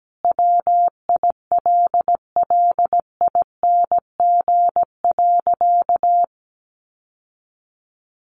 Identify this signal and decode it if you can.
Morse code, heard as a keyed tone, about 17 wpm, 700 Hz. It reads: WILLING.
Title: Sentences from Top 1000 Words 17wpm